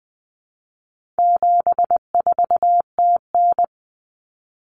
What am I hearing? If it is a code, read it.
Morse code, heard as a keyed tone, about 20 wpm, 700 Hz. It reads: 74TN
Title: Common Cut Numbers - 4-Digit 20wpm